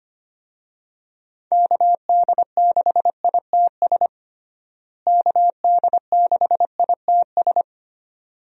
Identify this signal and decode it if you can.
Morse code, heard as a keyed tone, about 25 wpm, 700 Hz. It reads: KD6ITS KD6ITS
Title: USA Call Signs - 6 Characters - Repeated 2x - 25wpm